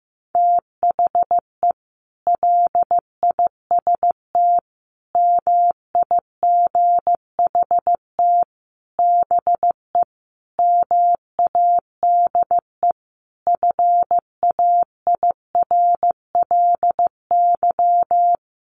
Morse code, heard as a keyed tone, about 15 wpm, 700 Hz. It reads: THE LIST MIGHT BE MADE FAIRLY
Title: Sentences from Top 1100 Words 15wpm